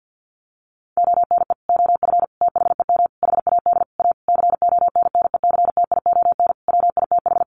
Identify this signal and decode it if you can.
Morse code, heard as a keyed tone, about 50 wpm, 700 Hz. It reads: ONE OF THEM HAD A CONNECTION WITH
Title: Sentences from Top 1300 Words 50wpm